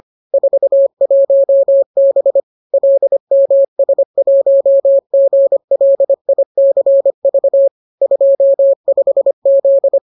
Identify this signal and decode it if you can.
Morse code, heard as a keyed tone, about 25 wpm, 550 Hz. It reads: 41B LMS1GLICV 25Z